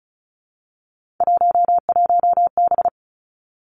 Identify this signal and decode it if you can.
Morse code, heard as a keyed tone, about 36 wpm, 700 Hz. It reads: 11B
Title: Three Challenging Letter-Number Combinations 35wpm